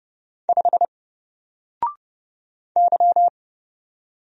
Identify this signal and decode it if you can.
Morse code, heard as a keyed tone, about 30 wpm, 700 Hz. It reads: 5 Y
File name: Single-Letter-Number-30wpm